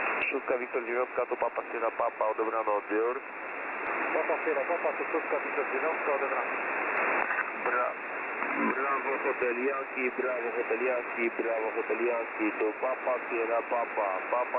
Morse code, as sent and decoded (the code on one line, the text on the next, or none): none